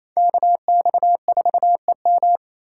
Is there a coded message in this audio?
Yes: Morse code from 0.2 to 2.4 s, heard as a keyed tone, about 36 wpm, 700 Hz. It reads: KX4EM